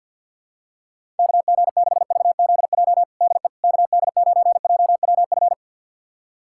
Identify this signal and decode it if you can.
Morse code, heard as a keyed tone, about 50 wpm, 700 Hz. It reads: KG6VZJ DE KD9JWF
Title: USA Callsign DE Callsign - 6 Characters - 50wpm